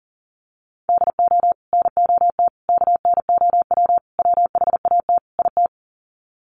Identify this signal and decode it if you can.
Morse code, heard as a keyed tone, about 39 wpm, 700 Hz. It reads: DO NOT KNOW WHAT IT
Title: Sets of 5 Words 40wpm